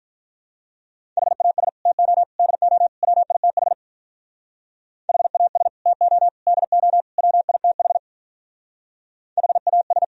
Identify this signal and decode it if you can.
Morse code, heard as a keyed tone, about 43 wpm, 700 Hz. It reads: HAS TO DO WITH HAS TO DO WITH HAS
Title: Speed-Racing - Sets of 4 Words 35wpm